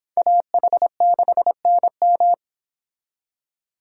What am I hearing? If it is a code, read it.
Morse code, heard as a keyed tone, about 25 wpm, 700 Hz. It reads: AH6NM